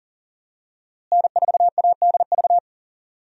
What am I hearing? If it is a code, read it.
Morse code, heard as a keyed tone, about 40 wpm, 700 Hz. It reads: N4ADV
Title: USA Call Signs - 5 Characters - Repeated 3x - 40wpm